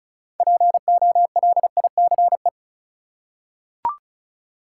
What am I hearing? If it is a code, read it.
Morse code, heard as a keyed tone, about 36 wpm, 700 Hz. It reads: POLICE
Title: Six Letter Words - 35wpm